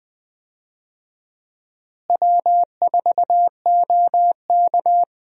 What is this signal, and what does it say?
Morse code, heard as a keyed tone, about 20 wpm, 700 Hz. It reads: W4OK
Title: USA Call Signs - 4 Characters - Repeated 3x - 20wpm